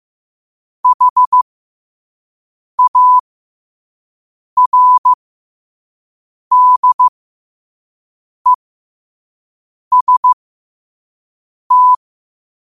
Morse code, heard as a keyed tone, about 15 wpm, spaced as for 5 wpm, 1000 Hz. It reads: HARDEST
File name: qotdfiles / 5wpm